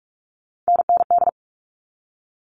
Morse code, heard as a keyed tone, about 44 wpm, 700 Hz. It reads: NND
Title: All Cut Numbers - 3-Digit 45wpm